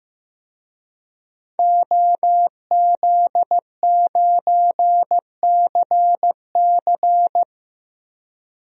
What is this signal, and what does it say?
Morse code, heard as a keyed tone, about 16 wpm, 700 Hz. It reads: OZ9CC
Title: DXCC-100 Call Signs - 5 Characters - Repeated 2x - 15wpm